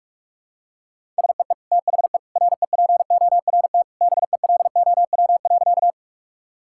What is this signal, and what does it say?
Morse code, heard as a keyed tone, about 42 wpm, 700 Hz. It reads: SEE THE REPORT BELOW.
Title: Sentences from Top 1000 Words - Review Part 6 - 45wpm